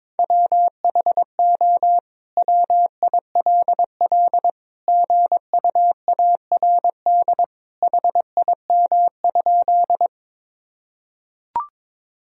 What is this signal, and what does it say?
Morse code, heard as a keyed tone, about 22 wpm, 700 Hz. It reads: WHO WILL GUARD HIM?
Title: Sentences from Top 1000 Words - Review Part 4 - 22wpm